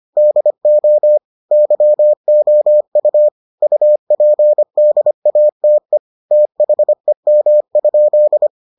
Morse code, heard as a keyed tone, about 25 wpm, 600 Hz. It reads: DO YOU UPDATE THEM?